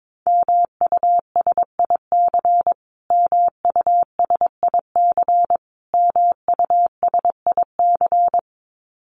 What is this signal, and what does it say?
Morse code, heard as a keyed tone, about 22 wpm, 700 Hz. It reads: MUSIC MUSIC MUSIC
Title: Top 500 Words Repeated 22wpm